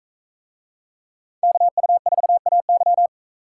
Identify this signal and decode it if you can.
Morse code, heard as a keyed tone, about 41 wpm, 700 Hz. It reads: KU4AY